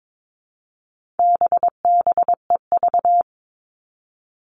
Morse code, heard as a keyed tone, about 22 wpm, 700 Hz. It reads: BBEV
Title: All Cut Numbers - 4-Digit 22wpm